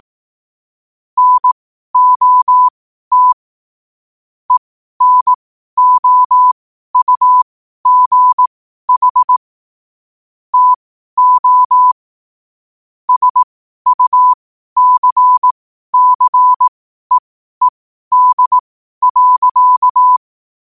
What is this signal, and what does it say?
Morse code, heard as a keyed tone, about 18 wpm, spaced as for 12 wpm, 1000 Hz. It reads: NOT ENOUGH TO SUCCEED.